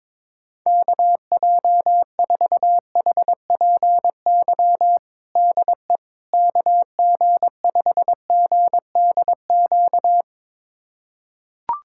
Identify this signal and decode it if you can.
Morse code, heard as a keyed tone, about 22 wpm, 700 Hz. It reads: KJ4HPY DE KG5GDQ
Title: USA Callsign DE Callsign 22wpm